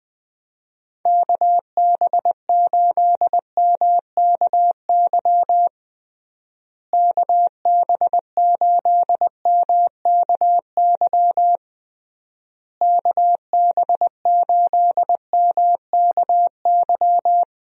Morse code, heard as a keyed tone, about 21 wpm, 700 Hz. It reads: KB8MKY KB8MKY KB8MKY
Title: USA Call Signs - 6 Characters - Repeated 3x - 20wpm